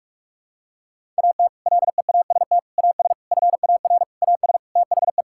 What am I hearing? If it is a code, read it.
Morse code, heard as a keyed tone, about 46 wpm, 700 Hz. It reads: AT LEAST AS FAR AS THE
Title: Sentences from Top 1000 Words - Review Part 6 - 45wpm